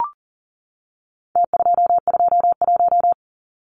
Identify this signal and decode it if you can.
Morse code, heard as a keyed tone, about 40 wpm, 700 Hz. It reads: T221
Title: Numbers - Contest Style - 4-Digit 40wpm